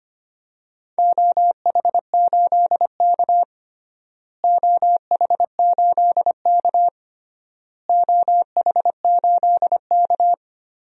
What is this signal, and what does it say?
Morse code, heard as a keyed tone, about 25 wpm, 700 Hz. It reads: OH8K OH8K OH8K
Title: DXCC-100 Call Signs - 4 Characters - Repeated 3x - 25wpm